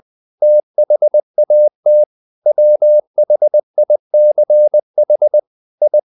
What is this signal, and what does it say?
Morse code, heard as a keyed tone, about 20 wpm, 600 Hz. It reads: THAT WHICH I
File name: the old man and the sea / the old man and the sea part 1 20wpm